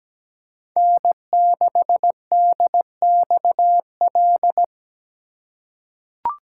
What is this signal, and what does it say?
Morse code, heard as a keyed tone, about 17 wpm, 700 Hz. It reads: N6DXL
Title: USA Call Signs - 5 Characters - 2x Character Spacing - 17wpm